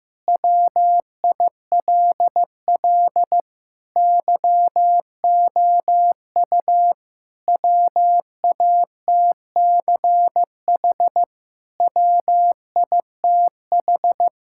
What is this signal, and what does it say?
Morse code, heard as a keyed tone, about 15 wpm, 700 Hz. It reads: WILL YOU WATCH WITH
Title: Sentences from Top 1000 Words - Review Part 5 - 15wpm